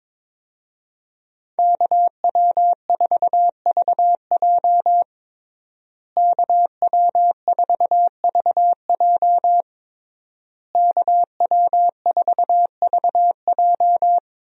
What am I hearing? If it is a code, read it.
Morse code, heard as a keyed tone, about 22 wpm, 700 Hz. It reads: KW4VJ KW4VJ KW4VJ
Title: USA Call Signs - 5 Characters - Repeated 3x - 22wpm